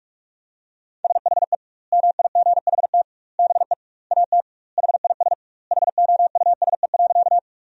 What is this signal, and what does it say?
Morse code, heard as a keyed tone, about 45 wpm, 700 Hz. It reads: SHE MIGHT BE AT HIS HOUSE.
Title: 2x Word-Spacing - Sentences from Top 300 Words - 45wpm